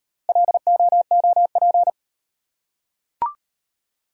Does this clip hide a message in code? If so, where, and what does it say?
Yes: Morse code from 0.3 to 1.9 s, heard as a keyed tone, about 35 wpm, 700 Hz. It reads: LOOP